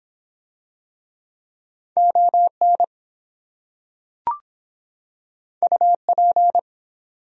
Morse code, heard as a keyed tone, about 23 wpm, 700 Hz. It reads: ON UP